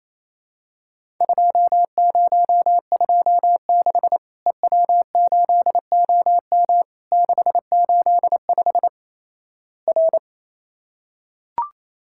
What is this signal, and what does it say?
Morse code, heard as a keyed tone, about 28 wpm, 700 Hz. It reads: 2026 EW8OM 685 R
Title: Worked All Europe DX Contest - QTC Exchange - 28wpm